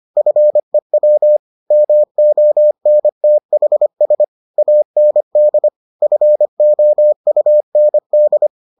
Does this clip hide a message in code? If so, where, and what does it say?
Yes: Morse code from 0.2 to 8.5 s, heard as a keyed tone, about 25 wpm, 600 Hz. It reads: FEW MONTHS AND FOUND